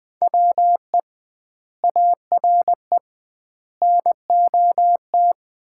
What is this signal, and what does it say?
Morse code, heard as a keyed tone, about 20 wpm, 700 Hz. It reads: WE ARE NOT
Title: 2x Word-Spacing - Sets of 5 Words - 20wpm